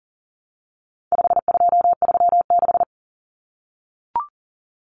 Morse code, heard as a keyed tone, about 40 wpm, 700 Hz. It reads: 5236
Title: Numbers - 4-Digit 40wpm